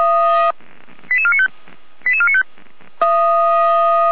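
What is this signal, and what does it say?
Five-tone selective call (ZVEI), heard as quick runs of five tones, tones of about 70 ms. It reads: 88495 88495
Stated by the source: ZVEI five-tone selective call received by SDR (455).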